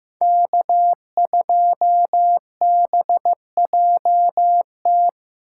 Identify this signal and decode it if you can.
Morse code, heard as a keyed tone, about 18 wpm, 700 Hz. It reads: K2BJT
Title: Speed Racing - USA 1x3 Call Signs 15wpm